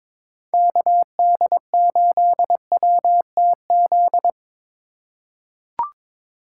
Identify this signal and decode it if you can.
Morse code, heard as a keyed tone, about 22 wpm, 700 Hz. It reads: KD8WTZ
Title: USA Call Signs - 2x Character Spacing - 22wpm